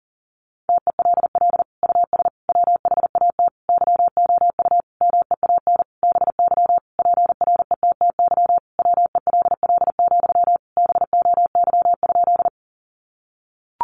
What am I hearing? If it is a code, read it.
Morse code, heard as a keyed tone, about 40 wpm, 700 Hz. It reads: TELL US WHAT YOU MEAN BY PRETTY WELL, BOY?
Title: Sentences from Top 1000 Words 40wpm